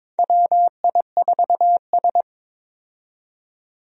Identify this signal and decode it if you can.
Morse code, heard as a keyed tone, about 22 wpm, 700 Hz. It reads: WI4S
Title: USA Call Signs - 4 Characters - 22wpm